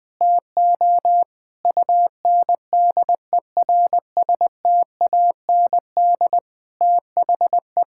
Morse code, heard as a keyed tone, about 20 wpm, 700 Hz. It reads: TO UNDERSTAND THE
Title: Sentences with 7-Word Phrases - 20wpm